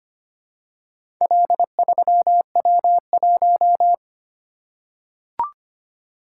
Morse code, heard as a keyed tone, about 25 wpm, 700 Hz. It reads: L3W1
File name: Four Letter-Number Combinations - 25wpm